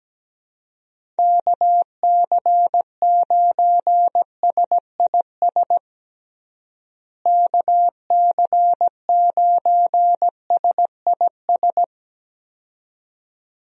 Morse code, heard as a keyed tone, about 17 wpm, 700 Hz. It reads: KC9SIS KC9SIS
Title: USA Call Signs - 6 Characters - Repeated 2x - 17wpm